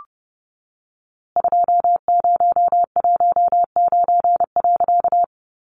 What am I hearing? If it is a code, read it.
Morse code, heard as a keyed tone, about 30 wpm, 700 Hz. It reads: 2019.
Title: A Sound Of Thunder 30wpm